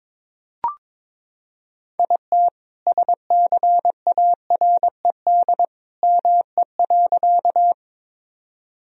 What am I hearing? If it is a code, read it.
Morse code, heard as a keyed tone, about 22 wpm, 700 Hz. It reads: IT SCARED ME.
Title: Sentences from Top 1000 Words - Review Part 4 - 22wpm